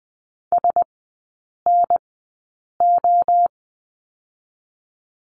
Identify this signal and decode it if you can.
Morse code, heard as a keyed tone, about 20 wpm, spaced as for 8 wpm, 700 Hz. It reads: SNO
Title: 081 Lesson 14e - Number 2 - Three Characters 20wpm